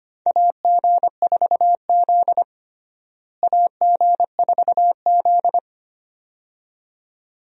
Morse code, heard as a keyed tone, about 25 wpm, 700 Hz. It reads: AG4Z AG4Z
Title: USA Call Signs - 4 Characters - Repeated 2x - 25wpm